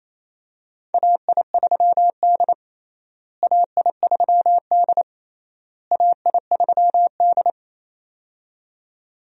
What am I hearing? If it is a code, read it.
Morse code, heard as a keyed tone, about 28 wpm, 700 Hz. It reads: AI3D AI3D AI3D
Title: USA Call Signs - 4 Characters - Repeated 3x - 28wpm